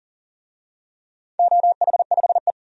Morse code, heard as a keyed tone, about 40 wpm, 690 Hz. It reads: OH5E
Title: DXCC-100 Call Signs - 4 Characters - 40wpm